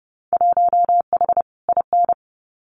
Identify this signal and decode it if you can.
Morse code, heard as a keyed tone, about 30 wpm, 700 Hz. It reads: 1H IN
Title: Winter Field Day Exchanges 30wpm